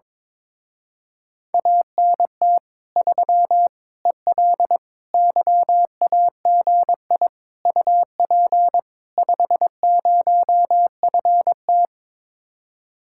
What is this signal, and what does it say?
Morse code, heard as a keyed tone, about 22 wpm, 700 Hz. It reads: ANT 3 EL YAGI UP 50FT
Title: Speed-Racing Common QSO Phrases Encore - 22wpm